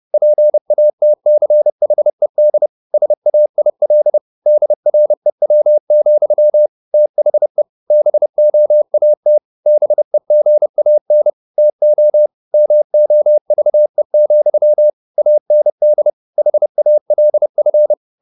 Morse code, heard as a keyed tone, about 30 wpm, 600 Hz. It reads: PATCHED SAIL DREW, THE BOAT BEGAN TO MOVE, AND HALF